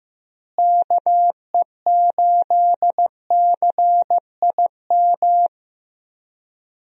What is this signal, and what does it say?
Morse code, heard as a keyed tone, about 15 wpm, 700 Hz. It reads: KE8CIM
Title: USA Call Signs - 6 Characters - Repeated 3x - 15wpm